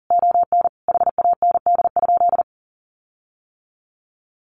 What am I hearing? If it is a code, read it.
Morse code, heard as a keyed tone, about 40 wpm, 700 Hz. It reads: ON HAND?
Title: Sentences from Top 1500 Words 40wpm